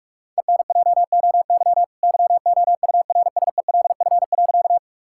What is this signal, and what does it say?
Morse code, heard as a keyed tone, about 41 wpm, 700 Hz. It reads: ENJOY YOURSELF.
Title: Sentences from Top 1000 Words - Review Part 5 - 45wpm